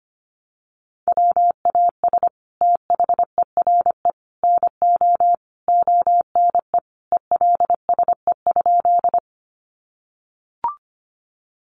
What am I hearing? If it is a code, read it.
Morse code, heard as a keyed tone, about 25 wpm, 700 Hz. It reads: WAS THERE NO ONE ELSE?
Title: Sentences from Top 800 Words 25wpm